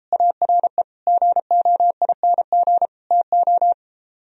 Morse code, heard as a keyed tone, about 34 wpm, 700 Hz. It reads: ARE GOING TO